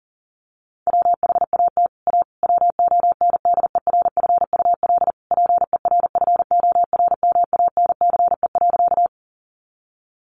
Morse code, heard as a keyed tone, about 40 wpm, 700 Hz. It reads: WHAT A WONDERFUL PERFORMANCE.
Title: Sentences from Top 1100 Words 40wpm